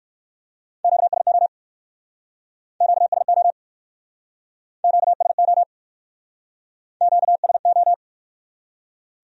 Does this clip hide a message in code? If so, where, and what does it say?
Yes: Morse code from 0.8 to 7.9 s, heard as a keyed tone, about 66 wpm, 700 Hz. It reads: QSO QSO QSO QSO